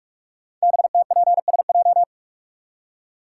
Morse code, heard as a keyed tone, about 44 wpm, 700 Hz. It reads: BTPSJ
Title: Code Groups - 5 Letters - 45wpm